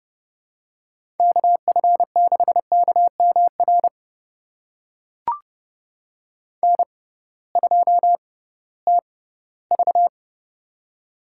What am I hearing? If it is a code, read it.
Morse code, heard as a keyed tone, about 31 wpm, spaced as for 18 wpm, 700 Hz. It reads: KF6KMR N2TV
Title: USA Call Signs - 6x Character Spacing - 30wpm